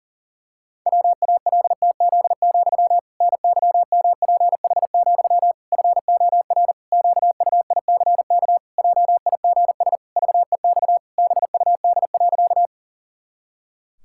Morse code, heard as a keyed tone, about 40 wpm, 700 Hz. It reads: WALTZ, NYMPH, FOR QUICK JIGS VEX BUD.
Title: Pangram Challenge v2 - 40wpm